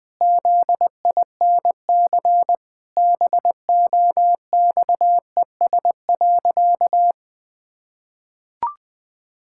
Morse code, heard as a keyed tone, about 21 wpm, 700 Hz. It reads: ZINC BOXES.
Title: Pangram Challenge v2 - 20wpm